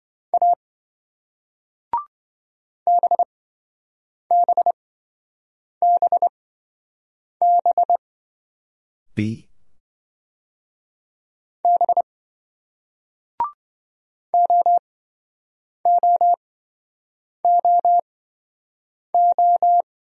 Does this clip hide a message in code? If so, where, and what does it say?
Yes: Morse code from 0.3 to 19.8 s, heard as a keyed tone, about 27 wpm, spaced as for 7 wpm, 700 Hz. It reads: A BBBB B OOOO